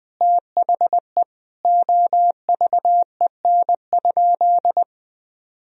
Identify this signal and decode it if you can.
Morse code, heard as a keyed tone, about 20 wpm, 700 Hz. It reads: THE OVEN?
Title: Sentences from Top 1000 Words - Review Part 4 - 20wpm